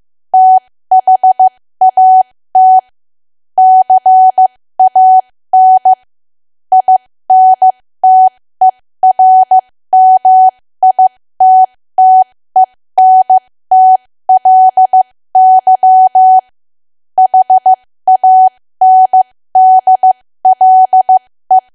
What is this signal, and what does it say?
Morse code, heard as a keyed tone, about 15 wpm, spaced as for 13 wpm, 750 Hz. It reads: THAT CAN INTERMITTENTLY HANDLE
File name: cwa / W1AW-Level2 / 13WPM